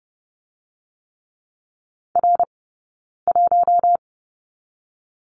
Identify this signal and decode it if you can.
Morse code, heard as a keyed tone, about 30 wpm, 700 Hz. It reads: R 1
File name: Lesson 18d - Letter m - Two Characters 30wpm